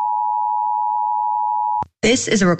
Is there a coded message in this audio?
Yes: Emergency Alert System (SAME) tones (the 853 and 960 Hz attention tone) from 0.0 to 1.9 s, with speech.